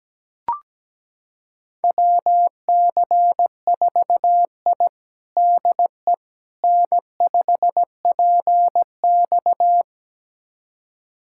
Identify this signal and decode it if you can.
Morse code, heard as a keyed tone, about 17 wpm, 700 Hz. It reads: WC4I DE N5PX
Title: USA Callsign DE Callsign - 4 Characters - 17wpm